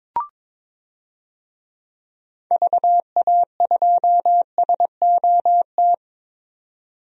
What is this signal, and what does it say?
Morse code, heard as a keyed tone, about 22 wpm, 700 Hz. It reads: VA2SOT
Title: DXCC-100 Call Signs - 6 Characters - 22wpm